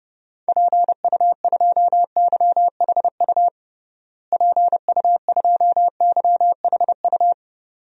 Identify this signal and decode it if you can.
Morse code, heard as a keyed tone, about 30 wpm, 700 Hz. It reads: PU2YHU PU2YHU
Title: DXCC-100 Call Signs - 6 Characters - Repeated 2x - 30wpm